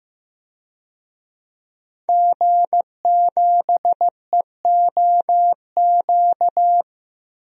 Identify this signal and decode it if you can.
Morse code, heard as a keyed tone, about 15 wpm, 700 Hz. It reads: G7EOQ
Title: DXCC-100 Call Signs - 5 Characters - Repeated 2x - 15wpm